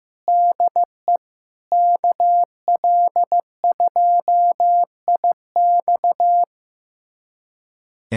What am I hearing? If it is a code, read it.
Morse code, heard as a keyed tone, about 15 wpm, 700 Hz. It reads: DE KL2IX
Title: USA Callsign DE Callsign - 5 Characters - 15wpm